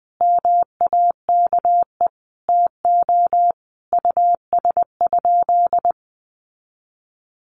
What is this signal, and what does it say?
Morse code, heard as a keyed tone, about 20 wpm, 700 Hz. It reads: MAKE TO US?
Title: Sentences from Top 1400 Words 20wpm